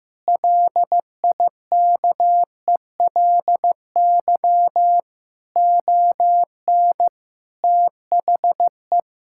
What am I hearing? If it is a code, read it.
Morse code, heard as a keyed tone, about 16 wpm, 700 Hz. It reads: LIKELY ON THE
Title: Sentences from Top 1000 Words - Review Part 5 - 15wpm